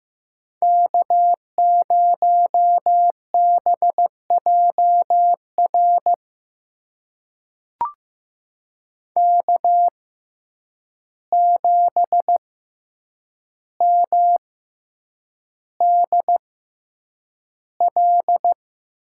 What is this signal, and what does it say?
Morse code, heard as a keyed tone, about 15 wpm, spaced as for 8 wpm, 700 Hz. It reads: K0BJR K7MDL
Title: USA Call Signs - 5 Characters - 6x Character Spacing - 15wpm